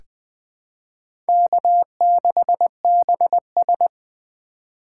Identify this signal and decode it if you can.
Morse code, heard as a keyed tone, about 20 wpm, 700 Hz. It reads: K6BS
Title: USA Call Signs - 4 Characters - 4x Character Spacing - 20wpm